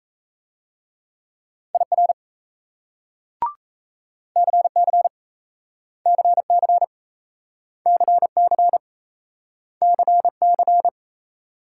Morse code, heard as a keyed tone, about 38 wpm, 700 Hz. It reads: IR CC CC CC CC